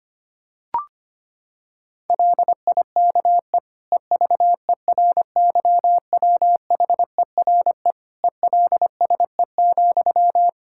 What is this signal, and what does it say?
Morse code, heard as a keyed tone, about 25 wpm, 700 Hz. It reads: LIKE EVERYWHERE ELSE,
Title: Sentences from Top 1600 Words 25wpm